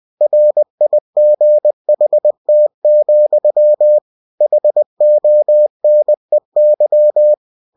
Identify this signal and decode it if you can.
Morse code, heard as a keyed tone, about 20 wpm, 600 Hz. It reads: RIGHT, HONEY
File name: cw-books / winnie-the-pooh / winnie-20wpm-page-0037